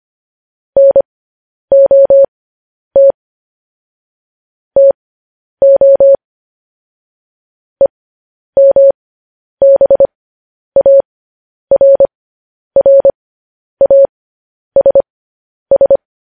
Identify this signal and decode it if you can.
Morse code, heard as a keyed tone, about 25 wpm, spaced as for 10 wpm, 550 Hz. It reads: NOT TO EMBARRASS